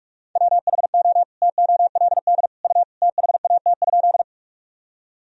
Morse code, heard as a keyed tone, about 46 wpm, 700 Hz. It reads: WHO TOLD U THAT?
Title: Sentences from Top 400 Words and Abbreviations 45wpm